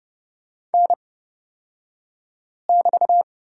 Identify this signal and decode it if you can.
Morse code, heard as a keyed tone, about 30 wpm, spaced as for 5 wpm, 700 Hz. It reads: N<BT>
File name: Lesson 36d - BT - Two Characters 30wpm